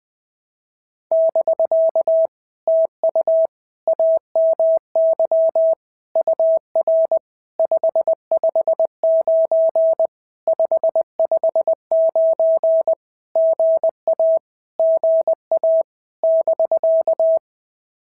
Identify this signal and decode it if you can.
Morse code, heard as a keyed tone, about 19 wpm, 650 Hz. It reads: <BK> TU AMY UR 559 559 GA GA <BK>
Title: POTA QSO Practice - 20wpm